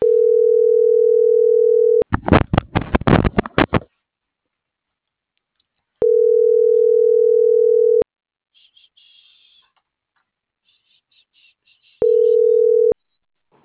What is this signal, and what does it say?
Ringback tone (North America), 440 + 480 Hz.